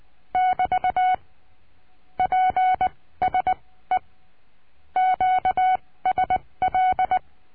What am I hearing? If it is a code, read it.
Morse code, heard as a keyed tone, about 20 wpm, spaced as for 15 wpm, 740 Hz. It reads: <BT> PSE QSL
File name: cursos / CURSO DE TELEGRAFIA / Telegrafia - Trilha 22